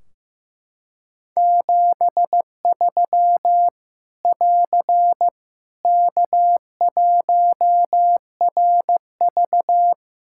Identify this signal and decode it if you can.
Morse code, heard as a keyed tone, about 15 wpm, 700 Hz. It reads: 73 <AR> K1RV
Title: Practice QSO - Exchange 4 - Ending - 15wpm